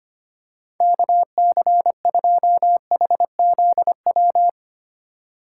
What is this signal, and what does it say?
Morse code, heard as a keyed tone, about 25 wpm, 700 Hz. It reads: KC2HZW
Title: USA Call Signs - 2x Character Spacing - 25wpm